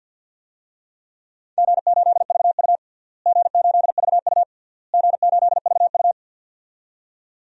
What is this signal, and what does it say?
Morse code, heard as a keyed tone, about 51 wpm, 700 Hz. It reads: G8VU G8VU G8VU
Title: DXCC-100 Call Signs - 4 Characters - Repeated 3x - 50wpm